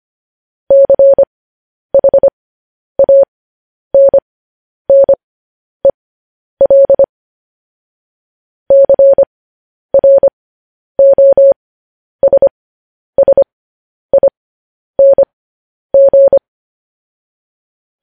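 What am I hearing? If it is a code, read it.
Morse code, heard as a keyed tone, about 25 wpm, spaced as for 10 wpm, 550 Hz. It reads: CHANNEL CROSSING
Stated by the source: Morse code news headlines sent as a reading transcription, preceded by vvv.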